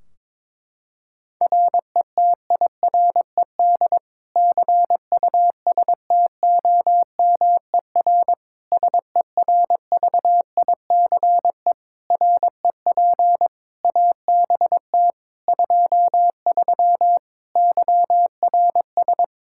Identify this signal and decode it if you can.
Morse code, heard as a keyed tone, about 21 wpm, 700 Hz. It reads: RETIRED CUSTOMER SERVICE REP ABT 23 YRS
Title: K1USN SST QSO Practice 22wpm